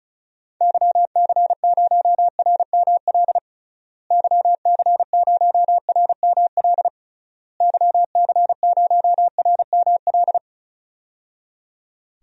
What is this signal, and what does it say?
Morse code, heard as a keyed tone, about 36 wpm, 700 Hz. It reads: YC0RML YC0RML YC0RML
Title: DXCC-100 Call Signs - 6 Characters - Repeated 3x - 35wpm